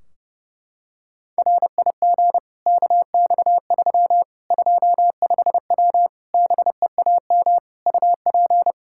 Morse code, heard as a keyed tone, about 32 wpm, 700 Hz. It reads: RIG KX3 25W BEAM UP
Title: K1USN SST QSO Practice 30wpm